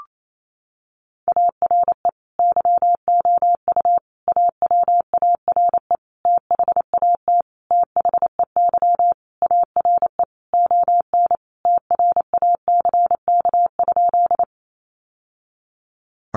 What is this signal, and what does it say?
Morse code, heard as a keyed tone, about 28 wpm, 700 Hz. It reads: ARE YOU AWARE THAT THEY ARE ON TRACK?
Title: Sentences from Top 1200 Words 28wpm